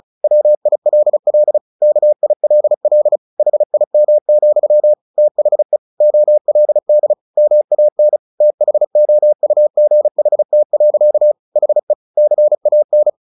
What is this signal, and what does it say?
Morse code, heard as a keyed tone, about 34 wpm, 600 Hz. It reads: WILL KILL HIM, THE OLD MAN THOUGHT. HE CAN
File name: the old man and the sea / the old man and the sea 1 35wpm